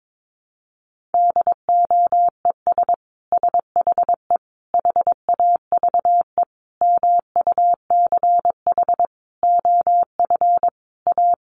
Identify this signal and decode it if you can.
Morse code, heard as a keyed tone, about 22 wpm, 700 Hz. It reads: DOES SHE HAVE MUCH OF A
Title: Sentences from Top 1400 Words 22wpm